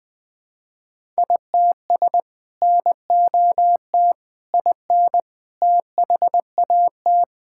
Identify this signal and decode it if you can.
Morse code, heard as a keyed tone, about 20 wpm, 700 Hz. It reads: ITS NOT IN THAT
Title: Sentences from Top 1000 Words - Review Part 5 - 20wpm